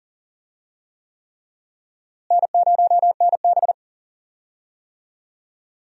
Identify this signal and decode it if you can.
Morse code, heard as a keyed tone, about 40 wpm, 700 Hz. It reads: N0NB
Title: USA Call Signs - 4 Characters - Repeated 3x - 40wpm